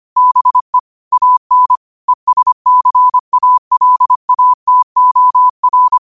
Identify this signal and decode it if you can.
Morse code, heard as a keyed tone, about 25 wpm, 1000 Hz. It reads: BE AN ESCALATOR
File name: qotdfiles / 25wpm